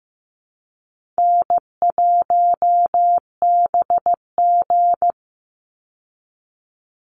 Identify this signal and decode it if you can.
Morse code, heard as a keyed tone, about 15 wpm, 700 Hz. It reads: N1BG
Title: Easy Call Signs Repeated 3x - 15wpm